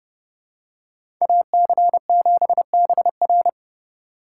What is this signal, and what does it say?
Morse code, heard as a keyed tone, about 28 wpm, 700 Hz. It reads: AC7BR